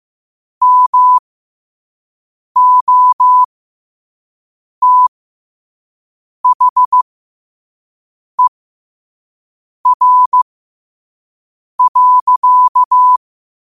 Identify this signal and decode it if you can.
Morse code, heard as a keyed tone, about 15 wpm, spaced as for 5 wpm, 1000 Hz. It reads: MOTHER.
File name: qotdfiles / 5wpm